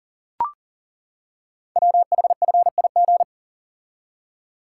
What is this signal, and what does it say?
Morse code, heard as a keyed tone, about 43 wpm, 690 Hz. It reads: WHFIG
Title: Code Groups - 5 Letters - 40wpm